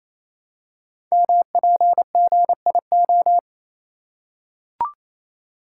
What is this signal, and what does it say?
Morse code, heard as a keyed tone, about 28 wpm, 700 Hz. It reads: MPGIO
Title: Code Groups - 5 Letters - 28wpm